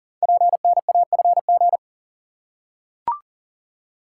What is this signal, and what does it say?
Morse code, heard as a keyed tone, about 37 wpm, 700 Hz. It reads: PNAFG
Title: Code Groups - 5 Letters - 40wpm